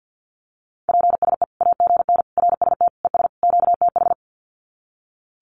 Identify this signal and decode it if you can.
Morse code, heard as a keyed tone, about 50 wpm, 700 Hz. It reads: PSE AGN RST ES QTH
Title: Common QSO Phrases 50wpm